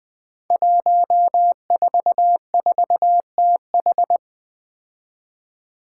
Morse code, heard as a keyed tone, about 20 wpm, 700 Hz. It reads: 144TH
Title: Five Challenging Letter-Number Combinations 20wpm